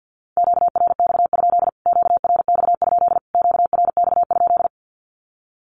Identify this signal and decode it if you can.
Morse code, heard as a keyed tone, about 48 wpm, 700 Hz. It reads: QRX? QRX? QRX?
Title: Q-Codes Repeated 3x 50wpm